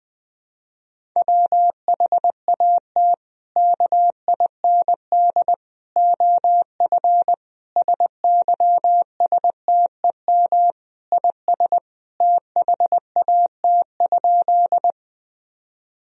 Morse code, heard as a keyed tone, about 21 wpm, 700 Hz. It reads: WHAT KIND OF SYSTEM IS THAT?
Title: Sentences from Top 1000 Words - Review Part 3 - 20wpm